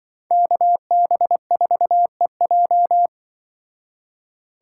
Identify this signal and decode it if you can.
Morse code, heard as a keyed tone, about 25 wpm, 700 Hz. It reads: KB4EJ
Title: Speed Racing - USA 2x2 Call Signs 20wpm